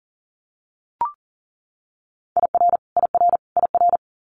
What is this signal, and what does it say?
Morse code, heard as a keyed tone, about 40 wpm, 700 Hz. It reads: IR IR IR
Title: Multi-letter Phonemes Repeated 3x 40wpm